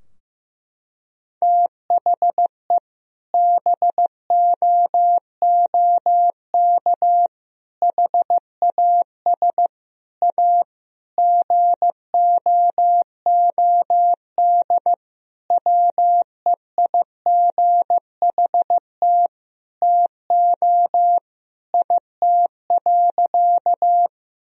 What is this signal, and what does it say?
Morse code, heard as a keyed tone, about 15 wpm, 700 Hz. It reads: THE BOOK HAS A GOOD WEIGHT TO IT.
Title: Sentences from Top 1000 Words - Review Part 3 - 15wpm